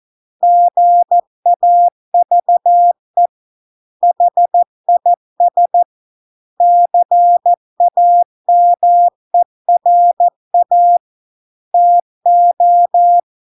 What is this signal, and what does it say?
Morse code, heard as a keyed tone, about 14 wpm, 700 Hz. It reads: GAVE HIS CAMERA TO